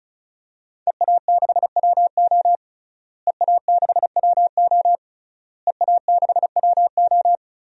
Morse code, heard as a keyed tone, about 36 wpm, 700 Hz. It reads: EA6WO EA6WO EA6WO
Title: DXCC-100 Call Signs - 5 Characters - Repeated 3x - 35wpm